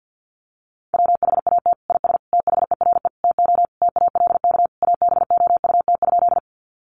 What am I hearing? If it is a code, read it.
Morse code, heard as a keyed tone, about 50 wpm, 700 Hz. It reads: WHAT IS THERE TO TALK ABOUT?
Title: Sentences from Top 400 Words 50wpm